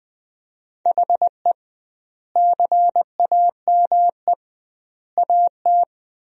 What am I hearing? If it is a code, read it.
Morse code, heard as a keyed tone, about 19 wpm, 700 Hz. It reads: HE CAME AT
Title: 2x Word-Spacing - Sentences from Top 300 Words - 20wpm